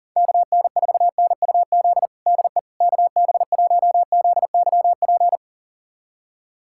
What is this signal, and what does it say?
Morse code, heard as a keyed tone, about 37 wpm, 700 Hz. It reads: KN4NUZ DE KB1ZYP
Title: USA Callsign DE Callsign 40wpm